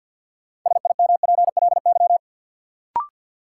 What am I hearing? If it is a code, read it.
Morse code, heard as a keyed tone, about 50 wpm, 700 Hz. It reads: SIMPLY
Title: Sentences from Top 600 Words - New Word Drill - 50wpm